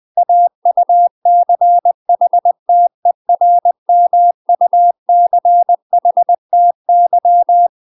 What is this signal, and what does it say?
Morse code, heard as a keyed tone, about 20 wpm, 700 Hz. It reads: AUCHTERMUCHTY